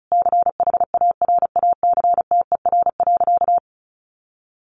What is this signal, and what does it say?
Morse code, heard as a keyed tone, about 34 wpm, 700 Hz. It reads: CHARACTER.
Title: Sentences from Top 1200 Words 35wpm